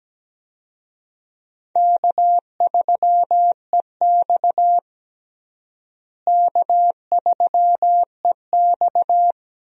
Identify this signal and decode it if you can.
Morse code, heard as a keyed tone, about 17 wpm, 700 Hz. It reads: K3EX K3EX
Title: USA Call Signs - 4 Characters - Repeated 3x - 17wpm